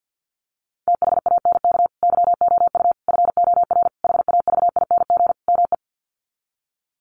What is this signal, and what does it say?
Morse code, heard as a keyed tone, about 48 wpm, 700 Hz. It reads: THANK YOU FOR HAVING ME
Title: Sets of 5 Words Encore 50wpm